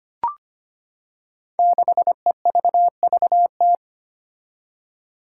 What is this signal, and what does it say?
Morse code, heard as a keyed tone, about 25 wpm, 700 Hz. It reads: 6EVVT
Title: Five Challenging Letter-Number Combinations 25wpm